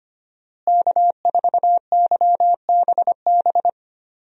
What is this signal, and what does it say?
Morse code, heard as a keyed tone, about 25 wpm, 700 Hz. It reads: K4YBB
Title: USA Call Signs - 5 Characters - Repeated 2x - 25wpm